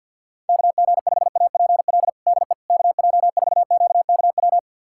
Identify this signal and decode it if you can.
Morse code, heard as a keyed tone, about 48 wpm, 690 Hz. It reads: KG5APL DE KJ4QKW
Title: USA Callsign DE Callsign 50wpm